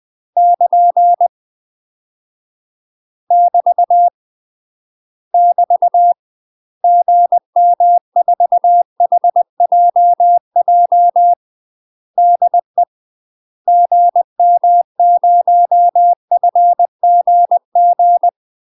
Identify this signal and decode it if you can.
Morse code, heard as a keyed tone, about 20 wpm, 700 Hz. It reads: <KN> <BT> <BT> GM4HJJ DE GM0FGG
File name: UK MASTER QSOs TRAINING / 20WPM / UK MASTER QSOs 0049